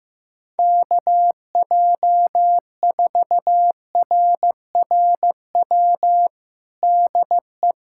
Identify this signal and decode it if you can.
Morse code, heard as a keyed tone, about 15 wpm, 700 Hz. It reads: KJ4RRW DE
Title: USA Callsign DE Callsign - 6 Characters - 15wpm